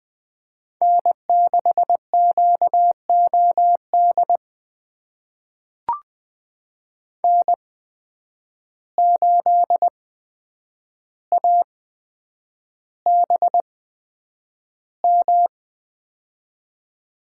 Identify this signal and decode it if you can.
Morse code, heard as a keyed tone, about 20 wpm, spaced as for 9 wpm, 700 Hz. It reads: N6QOD N8ABM
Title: USA Call Signs - 5 Characters - 8x Character Spacing - 20wpm